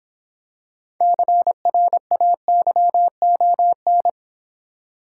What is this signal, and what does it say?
Morse code, heard as a keyed tone, about 26 wpm, 700 Hz. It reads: CRAYON